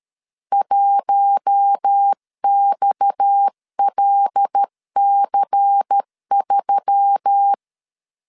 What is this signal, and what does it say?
Morse code, heard as a keyed tone, about 13 wpm, 800 Hz. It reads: 1XLC3
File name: CW / DET / 1 6 1500C S C 12wpm 3533